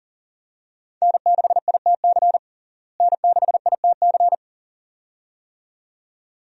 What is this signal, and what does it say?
Morse code, heard as a keyed tone, about 40 wpm, 700 Hz. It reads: N6ITC N6ITC
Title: USA Call Signs - 5 Characters - Repeated 2x - 40wpm